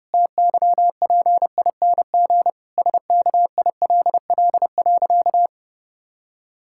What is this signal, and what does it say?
Morse code, heard as a keyed tone, about 30 wpm, 700 Hz. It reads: TYPING SKILL.
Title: Pangram Challenge v2 - 30wpm